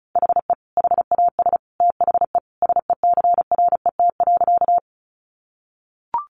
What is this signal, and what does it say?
Morse code, heard as a keyed tone, about 34 wpm, 700 Hz. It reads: HE HAS THE SECRET.
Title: Sentences from Top 1200 Words 35wpm